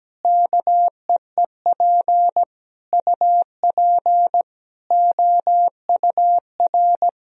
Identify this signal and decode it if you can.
Morse code, heard as a keyed tone, about 17 wpm, 700 Hz. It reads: KEEP UP OUR
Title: Sentences from Top 1000 Words - Review Part 4 - 17wpm